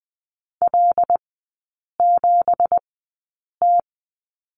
Morse code, heard as a keyed tone, about 20 wpm, spaced as for 8 wpm, 690 Hz. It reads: L7T
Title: 198 Lesson 32e - Letter k - Three Characters 20wpm